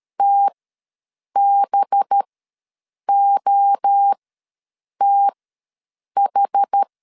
Morse code, heard as a keyed tone, about 13 wpm, spaced as for 7 wpm, 800 Hz. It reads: TBOTH
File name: LEZIONI / 5 LBVC / 5 LBVC ws 12wpm 3159